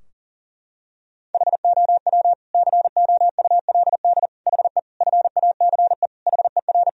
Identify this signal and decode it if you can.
Morse code, heard as a keyed tone, about 40 wpm, 700 Hz. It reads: HOW COULD HE FACE HER
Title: Sentences from Top 1000 Words - Review Part 2 - 40wpm